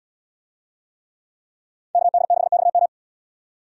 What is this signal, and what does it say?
Morse code, heard as a keyed tone, about 78 wpm, 690 Hz. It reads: KD6XM